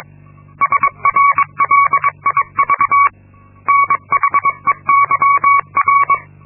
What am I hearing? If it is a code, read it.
Morse code, heard as a keyed tone, about 22 wpm, 1120 Hz. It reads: SRLIV NHEYR
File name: Linksmakalnis-2